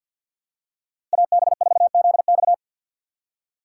Morse code, heard as a keyed tone, about 53 wpm, 700 Hz. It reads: AB4ZX
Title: USA Call Signs - 5 Characters - 8x Character Spacing - 50wpm